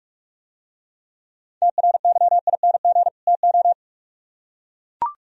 Morse code, heard as a keyed tone, about 44 wpm, 700 Hz. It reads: TRYING TO